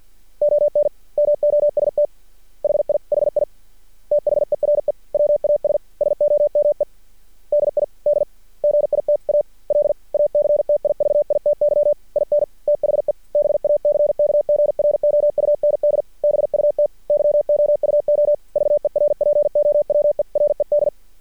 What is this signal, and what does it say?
Morse code, heard as a keyed tone, about 50 wpm, 590 Hz. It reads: ON MOST HI HI THERE WAS SOME DI DGITAL ACTIVITY IN THE BACKGROUND BUT YOU OVERPOWERED
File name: archived / CRSnet